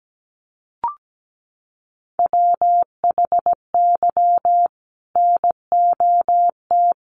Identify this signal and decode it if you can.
Morse code, heard as a keyed tone, about 17 wpm, 700 Hz. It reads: WHY NOT
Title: Sentences from Top 1500 Words 17wpm